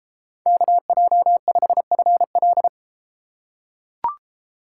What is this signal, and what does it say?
Morse code, heard as a keyed tone, about 32 wpm, 700 Hz. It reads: KJ5FL